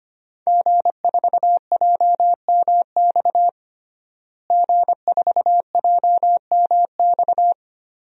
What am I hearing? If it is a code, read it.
Morse code, heard as a keyed tone, about 25 wpm, 700 Hz. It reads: G4JMX G4JMX
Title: DXCC-100 Call Signs - 5 Characters - Repeated 3x - 25wpm